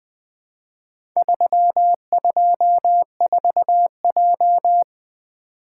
Morse code, heard as a keyed tone, about 20 wpm, 700 Hz. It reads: 324J
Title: Four Letter-Number Combinations - 20wpm